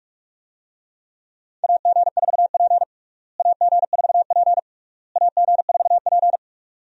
Morse code, heard as a keyed tone, about 46 wpm, 700 Hz. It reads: AG4P AG4P AG4P
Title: USA Call Signs - 4 Characters - Repeated 3x - 45wpm